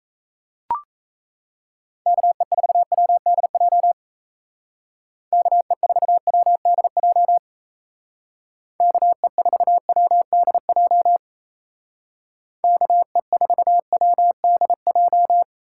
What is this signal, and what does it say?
Morse code, heard as a keyed tone, about 34 wpm, 700 Hz. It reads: KE4WDJ KE4WDJ KE4WDJ KE4WDJ
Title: Speed Racing - USA Call Signs 28wpm